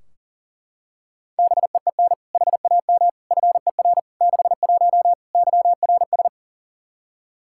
Morse code, heard as a keyed tone, about 40 wpm, 700 Hz. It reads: BEEN HAM FER 61 YRS
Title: Practice QSO - Exchange 3 - More Info - 40wpm